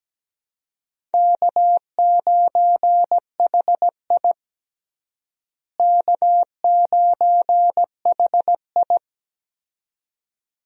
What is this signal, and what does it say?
Morse code, heard as a keyed tone, about 17 wpm, 700 Hz. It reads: K9HI K9HI
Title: USA Call Signs - 4 Characters - Repeated 2x - 17wpm